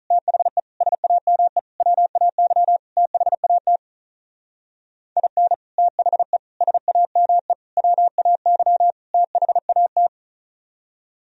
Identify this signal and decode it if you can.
Morse code, heard as a keyed tone, about 36 wpm, 700 Hz. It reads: THE SAME WAY THAT IN THE SAME WAY THAT
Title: Speed-Racing - Sets of 5 Words 35wpm